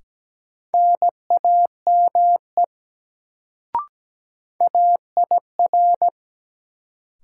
Morse code, heard as a keyed tone, about 17 wpm, 700 Hz. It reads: NAME AIR
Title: Sentences from Top 200 Words - New Word Drill - 17wpm